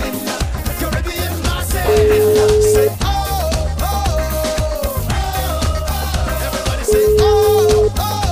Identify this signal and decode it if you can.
Ringback tone (Europe, 425 Hz), with noise.